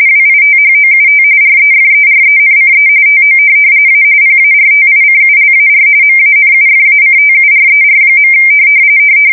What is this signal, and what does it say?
RTTY, about 45.45 baud. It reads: WITHOUT REFERENCE TO A PARTICULAR MANUFACTURER.
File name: fsk170 / teletype-RTTY